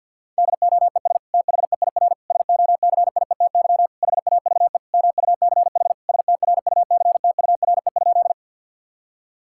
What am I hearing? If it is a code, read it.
Morse code, heard as a keyed tone, about 50 wpm, 700 Hz. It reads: DOES THEIR SOCIETY HAVE MUCH STRUCTURE?
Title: Sentences from Top 1000 Words - Review Part 6 - 50wpm